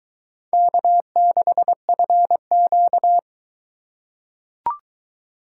Morse code, heard as a keyed tone, about 22 wpm, 700 Hz. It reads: K6FQ